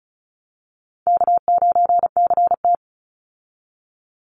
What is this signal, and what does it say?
Morse code, heard as a keyed tone, about 34 wpm, 700 Hz. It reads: K9CT
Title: Easy Call Signs Repeated 3x - 35wpm